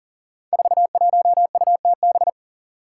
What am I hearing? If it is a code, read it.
Morse code, heard as a keyed tone, about 40 wpm, 700 Hz. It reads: 41UTB
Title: Five Challenging Letter-Number Combinations 40wpm